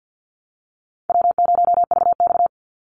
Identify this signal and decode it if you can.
Morse code, heard as a keyed tone, about 50 wpm, 700 Hz. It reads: W0VX
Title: Call Signs - 50wpm